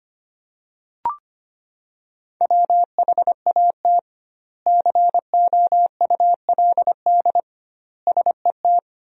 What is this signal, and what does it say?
Morse code, heard as a keyed tone, about 25 wpm, 700 Hz. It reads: WHAT COULD SET
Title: 2x Word-Spacing - Sentences from Top 200 Words - 25wpm